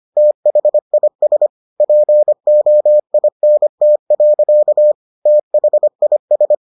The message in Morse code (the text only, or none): THIS POINT. THIS